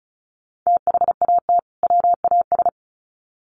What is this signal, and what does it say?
Morse code, heard as a keyed tone, about 34 wpm, 700 Hz. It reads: THAT WAS
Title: Sets of 2 Words 35wpm